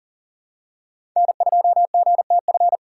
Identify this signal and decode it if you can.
Morse code, heard as a keyed tone, about 40 wpm, 700 Hz. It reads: N2GTF
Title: USA Call Signs - 5 Characters - 3x Character Spacing - 40wpm